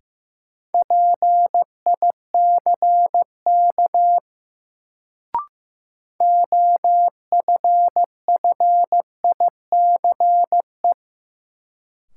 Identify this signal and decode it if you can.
Morse code, heard as a keyed tone, about 15 wpm, 700 Hz. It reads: PICK OFFICE
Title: Sentences from Top 1000 Words - New Word Drill - 15wpm